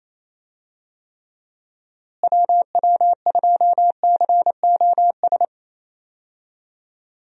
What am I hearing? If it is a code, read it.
Morse code, heard as a keyed tone, about 28 wpm, 700 Hz. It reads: WW2COS
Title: USA Call Signs - 6 Characters - Repeated 3x - 28wpm